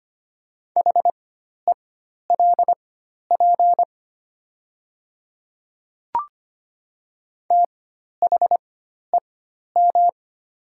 Morse code, heard as a keyed tone, about 25 wpm, spaced as for 12 wpm, 700 Hz. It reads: HELP THEM
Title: 4x Character Spacing - Top 300 Words - 25wpm